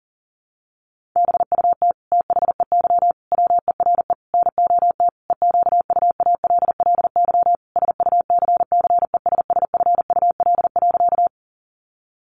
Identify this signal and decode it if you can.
Morse code, heard as a keyed tone, about 40 wpm, 700 Hz. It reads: BUT THEY WERE NOT EQUALLY SUCCESSFUL.
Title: Sentences from Top 1400 Words 40wpm